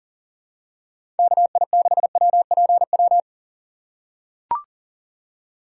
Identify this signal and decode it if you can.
Morse code, heard as a keyed tone, about 40 wpm, 700 Hz. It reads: KI6WPW
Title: USA Call Signs - 3x Character Spacing - 40wpm